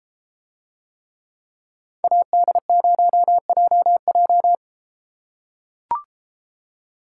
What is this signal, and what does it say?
Morse code, heard as a keyed tone, about 31 wpm, 700 Hz. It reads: AD0JJ